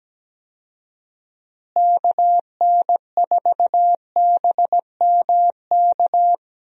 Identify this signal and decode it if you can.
Morse code, heard as a keyed tone, about 17 wpm, 700 Hz. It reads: KN4BMK
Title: USA Call Signs - 6 Characters - Repeated 3x - 17wpm